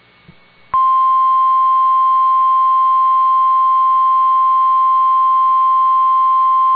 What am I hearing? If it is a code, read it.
Emergency Alert System (SAME) tones (the 1050 Hz attention tone). It is heard from 1.0 to 5.7 s.